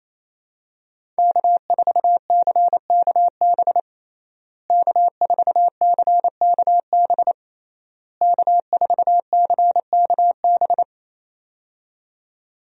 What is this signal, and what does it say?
Morse code, heard as a keyed tone, about 28 wpm, 700 Hz. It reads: K4CKB K4CKB K4CKB
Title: USA Call Signs - 5 Characters - Repeated 3x - 28wpm